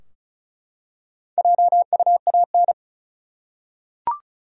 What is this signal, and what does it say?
Morse code, heard as a keyed tone, about 36 wpm, 700 Hz. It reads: JUAN
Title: USA Common Names - 35wpm